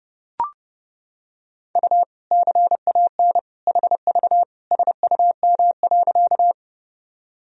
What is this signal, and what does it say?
Morse code, heard as a keyed tone, about 30 wpm, 700 Hz. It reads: U CAN HV SUM.
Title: Sentences from Top 100 Words and Abbreviations 30wpm